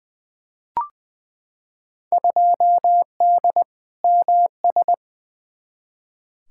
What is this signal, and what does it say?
Morse code, heard as a keyed tone, about 20 wpm, 700 Hz. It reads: 2D MS
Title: Field Day Exchanges Encore - 20wpm